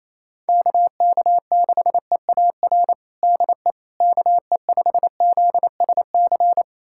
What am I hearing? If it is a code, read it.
Morse code, heard as a keyed tone, about 28 wpm, 700 Hz. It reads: KK6EAR DE KE5ZSC
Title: USA Callsign DE Callsign 28wpm